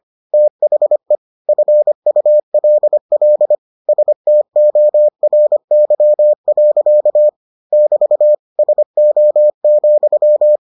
Morse code, heard as a keyed tone, about 25 wpm, 600 Hz. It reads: THE FULL STORY. <BT> SO,